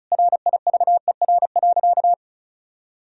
Morse code, heard as a keyed tone, about 36 wpm, 700 Hz. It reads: RIVER.
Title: Sentences from Top 1000 Words - Review Part 2 - 35wpm